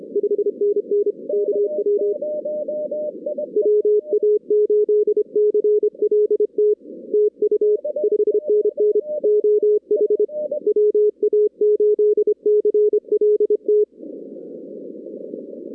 Morse code, heard as a keyed tone, about 26 wpm, 420 Hz. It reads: 5NN KW WA8CLT TU 5NN OH WA8CLT
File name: 2016 arrl dx cw / 2016-ARRL-DX-CW-K5TR- - 0-22